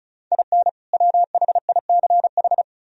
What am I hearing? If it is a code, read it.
Morse code, heard as a keyed tone, about 45 wpm, 700 Hz. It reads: IN WHICH